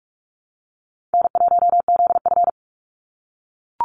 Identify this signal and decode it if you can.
Morse code, heard as a keyed tone, about 44 wpm, 700 Hz. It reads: N1ZF
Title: NA SOTA Activator Callsigns 45wpm